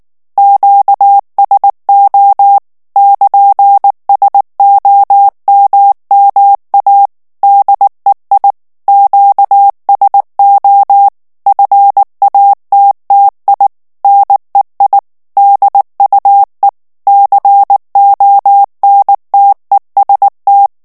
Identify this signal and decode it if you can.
Morse code, heard as a keyed tone, about 19 wpm, 800 Hz. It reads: QSO <KN>SOMMA DEI QSO FATTI NEI DUE CONTEST
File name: morse mp3 / ITALIEN texte clair 2 vitesse 18 mots minute